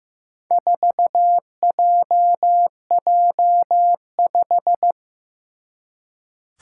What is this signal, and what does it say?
Morse code, heard as a keyed tone, about 15 wpm, 700 Hz. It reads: 4JJ5
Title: Four Challenging Letter-Number Combinations 15wpm